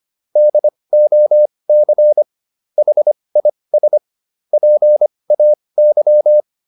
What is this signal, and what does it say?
Morse code, heard as a keyed tone, about 25 wpm, spaced as for 20 wpm, 600 Hz. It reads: DOC HIS PAY